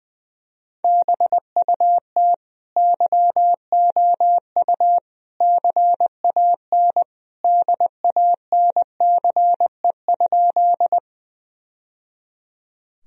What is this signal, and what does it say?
Morse code, heard as a keyed tone, about 20 wpm, 700 Hz. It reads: BUT YOU CAN DANCE?
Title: Sentences from Top 1000 Words - Review Part 4 - 20wpm